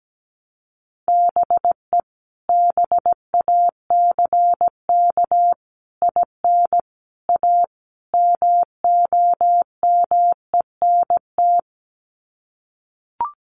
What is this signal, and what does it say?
Morse code, heard as a keyed tone, about 17 wpm, 700 Hz. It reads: BE BACK IN A MOMENT
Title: Sets of 5 Words 17wpm